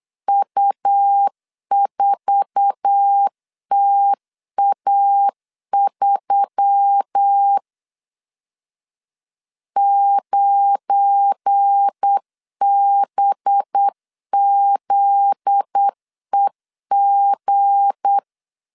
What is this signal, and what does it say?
Morse code, heard as a keyed tone, about 8 wpm, 800 Hz. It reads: U4TA3 9BZEG
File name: LEZIONI / 8 YJ45 / 8 YJ4 8wpm 3246